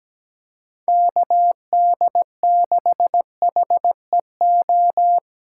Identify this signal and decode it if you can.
Morse code, heard as a keyed tone, about 17 wpm, 700 Hz. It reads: KD6HEO
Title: USA Call Signs - 6 Characters - 8x Character Spacing - 17wpm